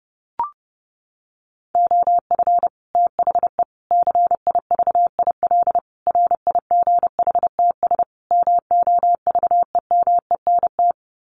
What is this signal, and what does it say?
Morse code, heard as a keyed tone, about 30 wpm, 700 Hz. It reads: OF THE CIVIL RIGHTS MOVEMENT
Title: Sets of 5 Words Encore2 30wpm